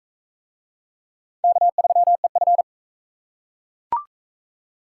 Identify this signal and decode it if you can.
Morse code, heard as a keyed tone, about 37 wpm, 700 Hz. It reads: K3EF